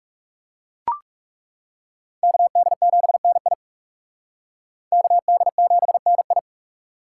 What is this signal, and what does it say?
Morse code, heard as a keyed tone, about 38 wpm, 700 Hz. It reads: KD7NI KD7NI